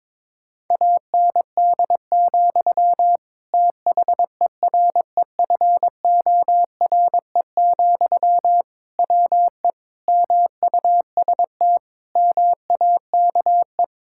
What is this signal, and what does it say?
Morse code, heard as a keyed tone, about 21 wpm, 700 Hz. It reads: AND, THEREFORE, WE MUST MAKE
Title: Sentences from Top 1000 Words - Review Part 5 - 22wpm